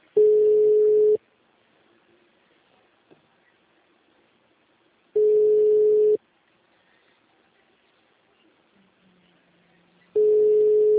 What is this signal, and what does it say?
Ringback tone (Europe, 425 Hz).
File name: out- -2005- - - .6292